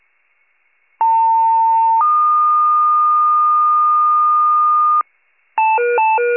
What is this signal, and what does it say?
Two-tone page, heard as a one-second tone, then a three-second tone, about 900 then 1220 Hz. It is heard from 1.0 to 5.0 s.
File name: pager / 94 RESCUE OFC / wcalert